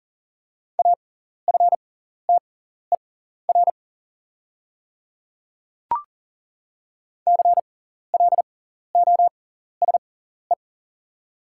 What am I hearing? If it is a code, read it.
Morse code, heard as a keyed tone, about 39 wpm, spaced as for 14 wpm, 700 Hz. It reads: AFTER CLOSE
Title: 6x Character Spacing - Top 500 Words - 40wpm